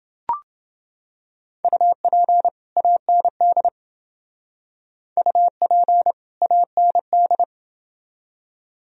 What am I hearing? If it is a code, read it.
Morse code, heard as a keyed tone, about 28 wpm, 700 Hz. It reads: UP AND UP AND